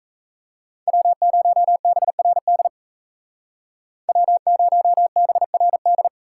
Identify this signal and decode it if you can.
Morse code, heard as a keyed tone, about 36 wpm, 700 Hz. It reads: W0BRD W0BRD